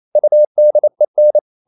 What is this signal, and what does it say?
Morse code, heard as a keyed tone, about 28 wpm, 600 Hz. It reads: UDEN
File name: Ferie QRQ 28wpm